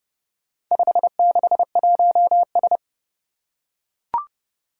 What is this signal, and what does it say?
Morse code, heard as a keyed tone, about 29 wpm, 700 Hz. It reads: 561S
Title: Four Challenging Letter-Number Combinations 30wpm